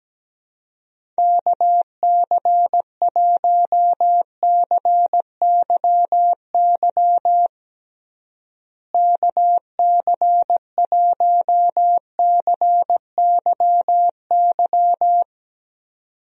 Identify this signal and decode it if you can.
Morse code, heard as a keyed tone, about 17 wpm, 700 Hz. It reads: KC1CYY KC1CYY
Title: USA Call Signs - 6 Characters - Repeated 2x - 17wpm